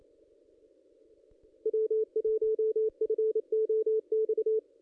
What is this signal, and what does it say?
Morse code, heard as a keyed tone, about 27 wpm, 440 Hz. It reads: W1FOX
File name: contests / ss-cw-k5tr-op-n5rz -1700